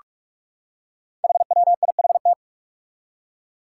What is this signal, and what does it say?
Morse code, heard as a keyed tone, about 44 wpm, 700 Hz. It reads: HWIHT
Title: Five Challenging Letter-Number Combinations 45wpm